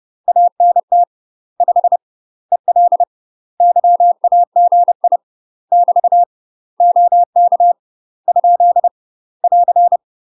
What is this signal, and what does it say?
Morse code, heard as a keyed tone, about 30 wpm, 700 Hz. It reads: ANT 5 EL YAGI <BT> OK ? <AR>
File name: UK MASTER QSOs TRAINING / 30WPM / UK MASTER QSOs 0002